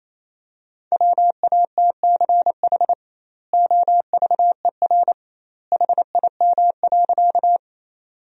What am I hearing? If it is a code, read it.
Morse code, heard as a keyed tone, about 28 wpm, 700 Hz. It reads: WATCH OVER HIM.
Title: 2x Word-Spacing - Sentences from Top 300 Words - 28wpm